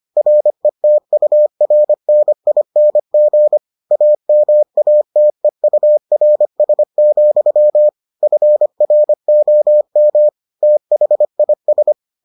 Morse code, heard as a keyed tone, about 25 wpm, 600 Hz. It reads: RETURNING AMATEURS, FROM THIS